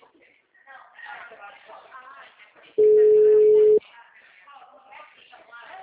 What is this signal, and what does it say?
Ringback tone (Europe, 425 Hz), with speech.
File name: out- -2010- - - .768